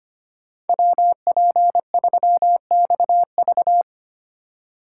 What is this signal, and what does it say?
Morse code, heard as a keyed tone, about 28 wpm, 700 Hz. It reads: WP3XV
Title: Speed Racing - USA 2x2 Call Signs 25wpm